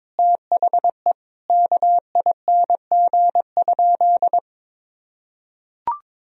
Morse code, heard as a keyed tone, about 22 wpm, 700 Hz. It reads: THE KING?
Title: Sentences from Top 1000 Words - Review Part 4 - 22wpm